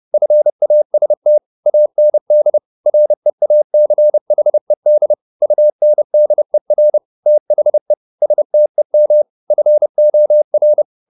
Morse code, heard as a keyed tone, about 30 wpm, 600 Hz. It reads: FAST AND REACHED UNDER THE STEM FOR